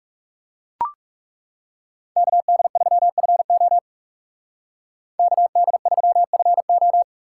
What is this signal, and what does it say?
Morse code, heard as a keyed tone, about 39 wpm, 700 Hz. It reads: KD3FO KD3FO